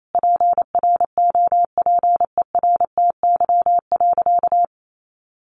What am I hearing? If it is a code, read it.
Morse code, heard as a keyed tone, about 28 wpm, 700 Hz. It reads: PROPERTY.
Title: Sentences from Top 1000 Words 28wpm